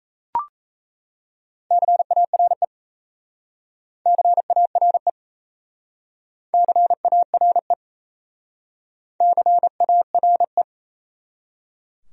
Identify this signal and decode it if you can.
Morse code, heard as a keyed tone, about 37 wpm, 700 Hz. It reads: CARE CARE CARE CARE